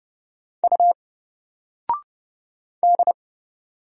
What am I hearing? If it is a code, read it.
Morse code, heard as a keyed tone, about 30 wpm, 700 Hz. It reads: U D
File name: Single-Letter-Number-30wpm